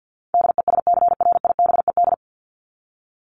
Morse code, heard as a keyed tone, about 50 wpm, 700 Hz. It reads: DESCRIBED
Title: ED Suffix - Top 100 Words with Base Words 50wpm